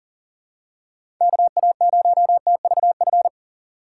Morse code, heard as a keyed tone, about 39 wpm, 700 Hz. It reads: KA0TVF
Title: USA Call Signs - 6 Characters - 6x Character Spacing - 40wpm